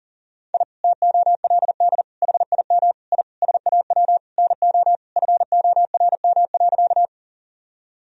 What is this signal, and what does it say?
Morse code, heard as a keyed tone, about 40 wpm, 700 Hz. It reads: I TOLD HIM I SAW NO FORM.
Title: Sentences from Top 1000 Words - Review Part 1 - 40wpm